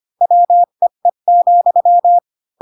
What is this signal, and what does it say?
Morse code, heard as a keyed tone, about 25 wpm, 700 Hz. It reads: WEE,
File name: Pinocchio / Pinocchio ch1 20 25 ik1hge